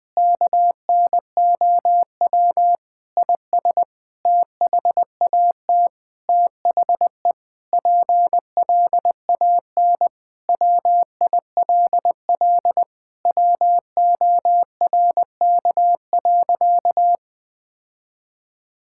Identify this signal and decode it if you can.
Morse code, heard as a keyed tone, about 20 wpm, 700 Hz. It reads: KNOW IS THAT THE PLAN WILL WORK.
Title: Sentences with 7-Word Phrases - Spoken Beginning - 20wpm